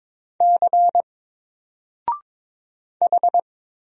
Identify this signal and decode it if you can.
Morse code, heard as a keyed tone, about 22 wpm, 700 Hz. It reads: C H
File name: Single-Letter-Number-22wpm